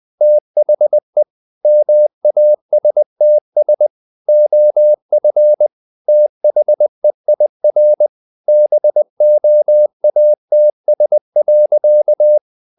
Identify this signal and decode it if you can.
Morse code, heard as a keyed tone, about 20 wpm, 600 Hz. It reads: THE MASTS OF THEIR BOATS.
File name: the old man and the sea / the old man and the sea part 1 20wpm